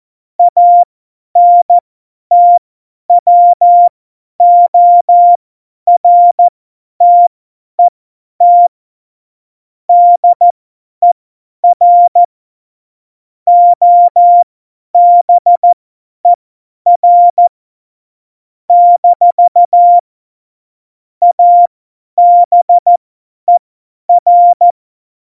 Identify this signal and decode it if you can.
Morse code, heard as a keyed tone, about 14 wpm, spaced as for 10 wpm, 700 Hz. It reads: ANTWORTET DER OBER - ABER
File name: Morsekurs / Witze / Witz 25. Juni 2019, 50 F70